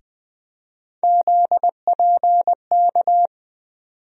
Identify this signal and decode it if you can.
Morse code, heard as a keyed tone, about 20 wpm, 700 Hz. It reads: ZPK
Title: USA Call Sign Postfixes - 3 Letters 20wpm